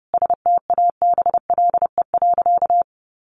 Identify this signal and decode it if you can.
Morse code, heard as a keyed tone, about 30 wpm, 700 Hz. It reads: STABLE.
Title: Sentences from Top 1500 Words 30wpm